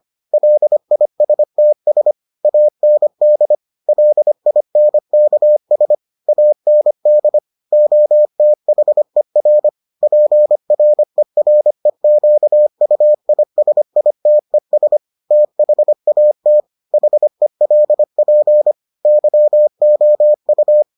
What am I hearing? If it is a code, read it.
Morse code, heard as a keyed tone, about 25 wpm, 600 Hz. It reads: LISTS AND LINKS AND OTHER PREREQUISITES THAT HELP YOU